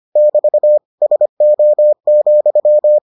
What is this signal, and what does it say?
Morse code, heard as a keyed tone, about 25 wpm, 600 Hz. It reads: <BT> SO,